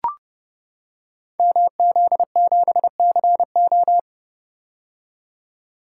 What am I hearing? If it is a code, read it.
Morse code, heard as a keyed tone, about 30 wpm, 700 Hz. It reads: MZ7CO
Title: Code Groups - 5 Letter:Numbers - 30wpm